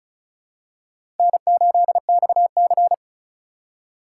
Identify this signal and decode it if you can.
Morse code, heard as a keyed tone, about 34 wpm, 700 Hz. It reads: N8XC
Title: USA Call Signs - 2x Character Spacing - 35wpm